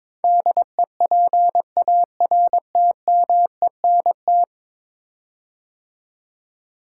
Morse code, heard as a keyed tone, about 22 wpm, 700 Hz. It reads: DEPARTMENT
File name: practice / 22wpm / 10 Letter Words - First 3 Letters Given - 22wpm